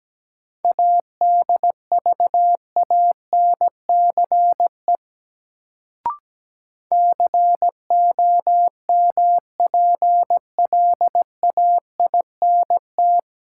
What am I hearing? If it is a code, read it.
Morse code, heard as a keyed tone, about 17 wpm, 700 Hz. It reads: ADVANCE COMPLAINT
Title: Sentences from Top 1400 Words - New Word Drill - 17wpm